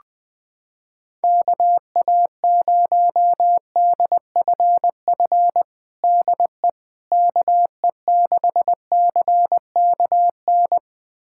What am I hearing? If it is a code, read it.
Morse code, heard as a keyed tone, about 20 wpm, 700 Hz. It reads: KA0DFF DE KE6CKN
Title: USA Callsign DE Callsign 20wpm